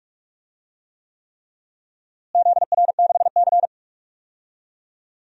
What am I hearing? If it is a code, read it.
Morse code, heard as a keyed tone, about 41 wpm, 700 Hz. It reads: ZR6C
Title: DXCC-100 Call Signs - 4 Characters - Repeated 3x - 45wpm